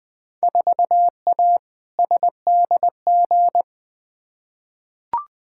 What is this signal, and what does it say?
Morse code, heard as a keyed tone, about 20 wpm, 700 Hz. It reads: 4A SDG
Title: Field Day Exchanges - 20wpm